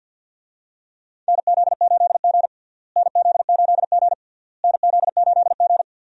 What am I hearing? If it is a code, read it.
Morse code, heard as a keyed tone, about 52 wpm, 700 Hz. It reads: NZ8G NZ8G NZ8G
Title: USA Call Signs - 4 Characters - Repeated 3x - 50wpm